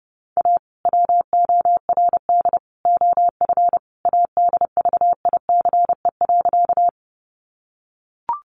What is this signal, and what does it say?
Morse code, heard as a keyed tone, about 30 wpm, 700 Hz. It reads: A WORD OF ADVICE.
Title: Sentences from Top 1100 Words 30wpm